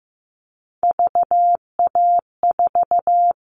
Morse code, heard as a keyed tone, about 15 wpm, 700 Hz. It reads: VA4
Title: All Cut Numbers - 3-Digit 15wpm